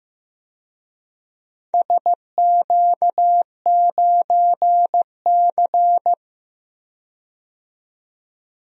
Morse code, heard as a keyed tone, about 15 wpm, 700 Hz. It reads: SQ9C
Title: DXCC-100 Call Signs - 4 Characters - 15wpm